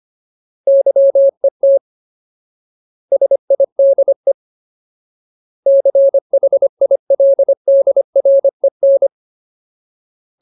Morse code, heard as a keyed tone, about 25 wpm, 550 Hz. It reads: YET SIDE CHILDREN
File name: Ultimate Word List / 25WPM Ultimate